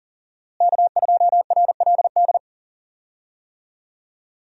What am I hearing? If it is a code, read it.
Morse code, heard as a keyed tone, about 40 wpm, 700 Hz. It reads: K2RLD
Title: USA Call Signs - 5 Characters - Repeated 2x - 40wpm